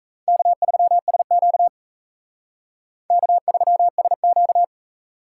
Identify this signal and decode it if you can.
Morse code, heard as a keyed tone, about 35 wpm, 700 Hz. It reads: K3SQ K3SQ
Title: Speed Racing - USA 1x2 Call Signs 28wpm